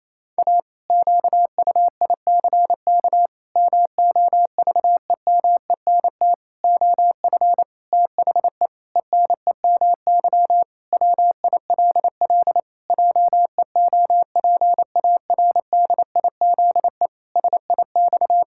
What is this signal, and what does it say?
Morse code, heard as a keyed tone, about 27 wpm, 700 Hz. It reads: A QUICK MOVEMENT OF THE ENEMY WILL JEOPARDIZE SIX
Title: Pangram Challenge v2 - 28wpm